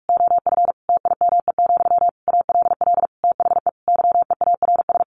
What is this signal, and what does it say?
Morse code, heard as a keyed tone, about 44 wpm, 700 Hz. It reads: OF TIME, ALL THE YEARS
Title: A Sound Of Thunder 45wpm